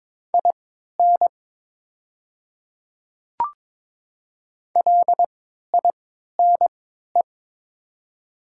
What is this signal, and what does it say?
Morse code, heard as a keyed tone, about 22 wpm, spaced as for 13 wpm, 690 Hz. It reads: IN LINE
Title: 3x Character Spacing - Top 200 Words - 22wpm